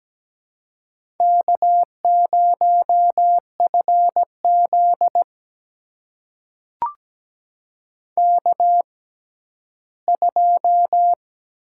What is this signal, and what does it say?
Morse code, heard as a keyed tone, about 17 wpm, spaced as for 11 wpm, 700 Hz. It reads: K0FZ K2
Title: USA Call Signs - 4 Characters - 6x Character Spacing - 17wpm